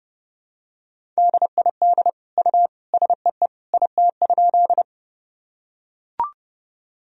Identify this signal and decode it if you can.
Morse code, heard as a keyed tone, about 30 wpm, 700 Hz. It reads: DID U SEE IT?
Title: Sentences from Top 100 Words and Abbreviations 30wpm